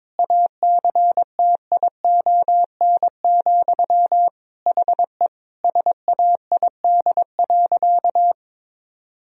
Morse code, heard as a keyed tone, about 22 wpm, 700 Hz. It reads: ACTION, HE SAID.
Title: Sentences from Top 1000 Words - Review Part 4 - 22wpm